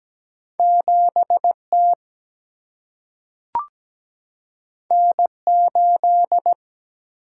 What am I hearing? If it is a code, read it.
Morse code, heard as a keyed tone, about 17 wpm, 700 Hz. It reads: 7T N8
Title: Single-Letter Single-Number 17wpm